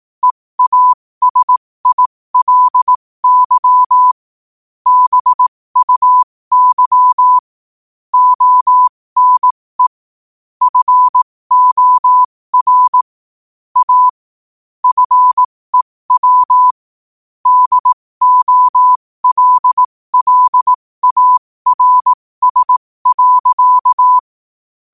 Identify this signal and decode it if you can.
Morse code, heard as a keyed tone, about 18 wpm, spaced as for 15 wpm, 1000 Hz. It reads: EASILY BUY ONE FOR A FEW DOLLARS.
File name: qotdfiles / 15wpm